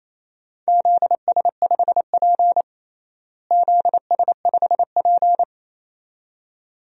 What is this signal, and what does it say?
Morse code, heard as a keyed tone, about 27 wpm, 700 Hz. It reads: ZS5P ZS5P
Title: DXCC-100 Call Signs - 4 Characters - Repeated 2x - 28wpm